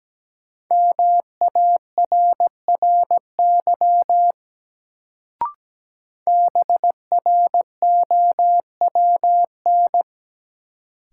Morse code, heard as a keyed tone, about 17 wpm, 700 Hz. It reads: MARRY BROWN
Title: Sentences from Top 1200 Words - New Word Drill Alt - 17wpm